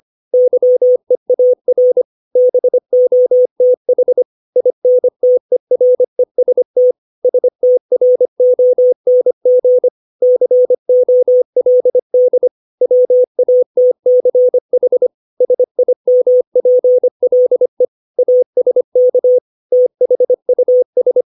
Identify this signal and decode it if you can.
Morse code, heard as a keyed tone, about 25 wpm, 500 Hz. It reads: YEAR BOTH INTEREST STRONG COLD WATCH SIMPLE ASK THUS
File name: 632 words / 632 Words 25WPM